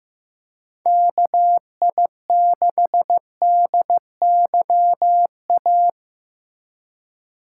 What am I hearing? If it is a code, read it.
Morse code, heard as a keyed tone, about 15 wpm, 700 Hz. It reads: KI6DYA
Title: USA Call Signs - 2x Character Spacing - 15wpm